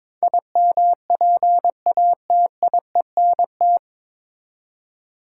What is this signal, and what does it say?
Morse code, heard as a keyed tone, about 21 wpm, 700 Hz. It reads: IMPATIENT
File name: 9 Letter Words - First 3 Letters Given - 22wpm